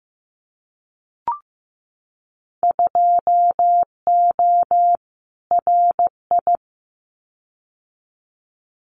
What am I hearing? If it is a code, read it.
Morse code, heard as a keyed tone, about 15 wpm, 700 Hz. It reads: 2O RI
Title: Winter Field Day Exchanges 15wpm